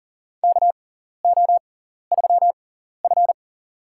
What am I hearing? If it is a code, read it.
Morse code, heard as a keyed tone, about 40 wpm, spaced as for 14 wpm, 700 Hz. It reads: KO3F
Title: USA Call Signs - 4 Characters - 6x Character Spacing - 40wpm